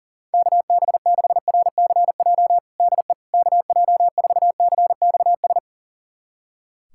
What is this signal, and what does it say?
Morse code, heard as a keyed tone, about 40 wpm, 700 Hz. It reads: KB6RCJ DE KJ4CXS
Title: USA Callsign DE Callsign - 6 Characters - 40wpm